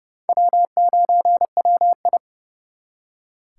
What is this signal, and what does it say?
Morse code, heard as a keyed tone, about 40 wpm, 700 Hz. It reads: W9WI